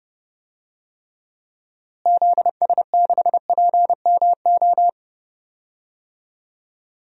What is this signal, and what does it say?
Morse code, heard as a keyed tone, about 30 wpm, 700 Hz. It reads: ZS6PMO
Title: DXCC-100 Call Signs - 6 Characters - Repeated 2x - 30wpm